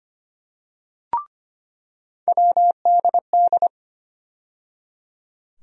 Morse code, heard as a keyed tone, about 25 wpm, 700 Hz. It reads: WDD
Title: USA Call Sign Postfixes - 3 Letters 25wpm